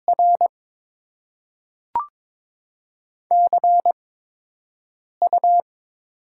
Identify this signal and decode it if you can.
Morse code, heard as a keyed tone, about 22 wpm, spaced as for 6 wpm, 700 Hz. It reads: R CU
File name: Top 20 QSO Elements - 8x Character Spacing - 22wpm